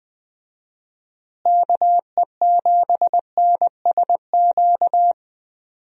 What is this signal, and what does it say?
Morse code, heard as a keyed tone, about 20 wpm, 700 Hz. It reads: KE7NSQ
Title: USA Call Signs - 6 Characters - Repeated 3x - 20wpm